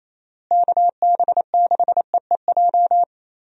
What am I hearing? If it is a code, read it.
Morse code, heard as a keyed tone, about 27 wpm, 700 Hz. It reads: KB6EEJ
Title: USA Call Signs - 6 Characters - Repeated 2x - 28wpm